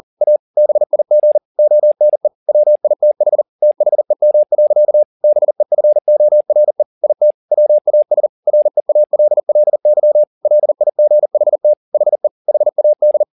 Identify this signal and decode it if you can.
Morse code, heard as a keyed tone, about 40 wpm, 600 Hz. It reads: A BIG ONE WITH THEM. BEFORE IT WAS REALLY LIGHT HE HAD